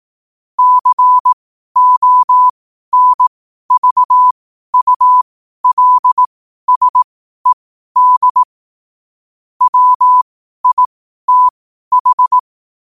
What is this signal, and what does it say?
Morse code, heard as a keyed tone, about 18 wpm, spaced as for 12 wpm, 1000 Hz. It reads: CONVULSED WITH
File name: qotdfiles / 12wpm